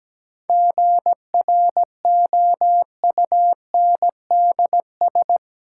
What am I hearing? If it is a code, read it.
Morse code, heard as a keyed tone, about 17 wpm, 700 Hz. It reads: GROUNDS
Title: Sentences from Top 1300 Words - New Word Drill Alt - 17wpm